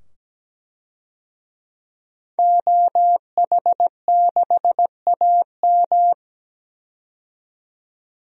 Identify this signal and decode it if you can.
Morse code, heard as a keyed tone, about 17 wpm, 700 Hz. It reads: OH6AM
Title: DXCC-100 Call Signs - 5 Characters - 17wpm